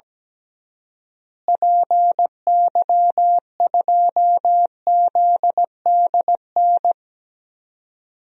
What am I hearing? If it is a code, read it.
Morse code, heard as a keyed tone, about 17 wpm, 700 Hz. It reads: PY2ZDN
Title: DXCC-100 Call Signs - 6 Characters - Repeated 3x - 17wpm